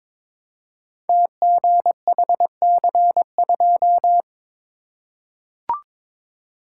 Morse code, heard as a keyed tone, about 22 wpm, 700 Hz. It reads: TGHC2
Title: Code Groups - 5 Letter:Numbers - 22wpm